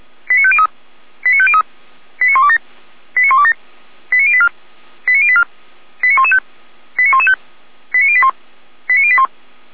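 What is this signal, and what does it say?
Five-tone selective call (ZVEI), heard as quick runs of five tones, tones of about 70 ms. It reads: 89503 89503 89127 89127 89084 89084 89115 89115 89081 89081
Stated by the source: ZVEI five-tone selective call 89143 received by SDR, from the Feuerwehr Orlamünde fire brigade server.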